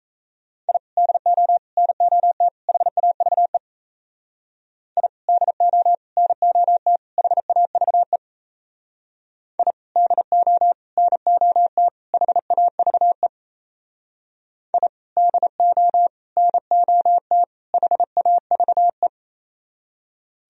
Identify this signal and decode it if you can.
Morse code, heard as a keyed tone, about 33 wpm, 700 Hz. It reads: I DO NOT HAVE I DO NOT HAVE I DO NOT HAVE I DO NOT HAVE
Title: Speed-Racing - Sets of 4 Words 28wpm